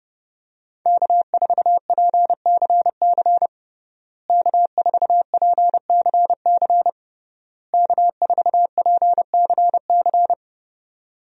Morse code, heard as a keyed tone, about 31 wpm, 700 Hz. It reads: K4PCC K4PCC K4PCC
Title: USA Call Signs - 5 Characters - Repeated 3x - 30wpm